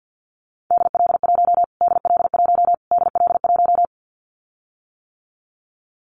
Morse code, heard as a keyed tone, about 50 wpm, 700 Hz. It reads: DL1 DL1 DL1
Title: Callsign Prefixes for Top 25 DXCC - Repeated 3x - 50wpm